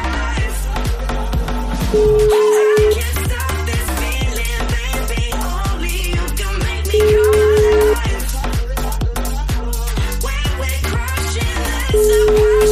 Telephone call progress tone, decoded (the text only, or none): ringback tone